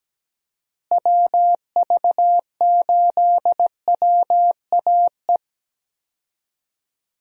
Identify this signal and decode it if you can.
Morse code, heard as a keyed tone, about 17 wpm, 700 Hz. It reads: WV8WAE
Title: USA Call Signs - 6 Characters - Repeated 2x - 17wpm